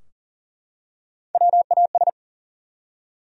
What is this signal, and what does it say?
Morse code, heard as a keyed tone, about 40 wpm, 700 Hz. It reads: WAS
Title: Top 100 Words 40wpm